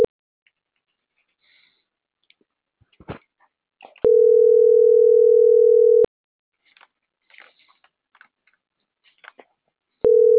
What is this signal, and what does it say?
Ringback tone (North America), 440 + 480 Hz.